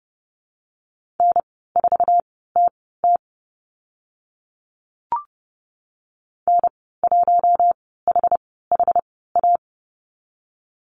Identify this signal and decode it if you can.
Morse code, heard as a keyed tone, about 30 wpm, spaced as for 17 wpm, 700 Hz. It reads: N4TT N1HHA
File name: Lesson 14g - Number 2 - Call Signs 30wpm